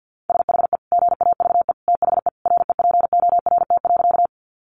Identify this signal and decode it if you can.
Morse code, heard as a keyed tone, about 48 wpm, 700 Hz. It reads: SHE GAVE THE REPORT.
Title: Sentences from Top 900 Words 50wpm